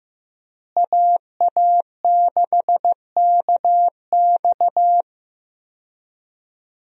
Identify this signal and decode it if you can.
Morse code, heard as a keyed tone, about 15 wpm, 700 Hz. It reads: AA6KX
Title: USA Call Signs - 5 Characters - Repeated 2x - 15wpm